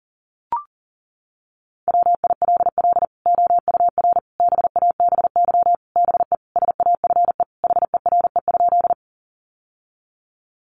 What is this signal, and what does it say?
Morse code, heard as a keyed tone, about 40 wpm, 700 Hz. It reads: WILL OUR BABY BE SAFE HERE?
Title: Sentences from Top 1300 Words 40wpm